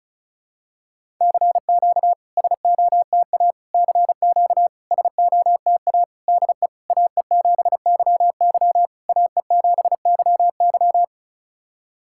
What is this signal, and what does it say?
Morse code, heard as a keyed tone, about 34 wpm, 700 Hz. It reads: CQ SOTA CQ SOTA DE AE7YY AE7YY
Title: SOTA QSO Practice 35wpm